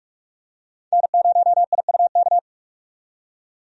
Morse code, heard as a keyed tone, about 46 wpm, 700 Hz. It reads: N0IUK
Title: USA Call Signs - 5 Characters - 45wpm